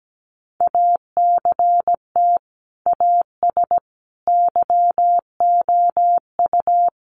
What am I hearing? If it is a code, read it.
Morse code, heard as a keyed tone, about 17 wpm, 700 Hz. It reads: ACT AS YOU
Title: Sentences from Top 1500 Words 17wpm